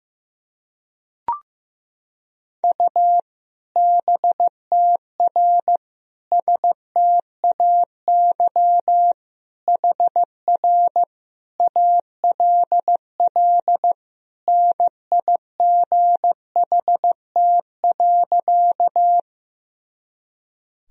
Morse code, heard as a keyed tone, about 15 wpm, 700 Hz. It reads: U BTR STAY HR ALL NIGHT.
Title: Sentences from Top 500 Words and Abbreviations 15wpm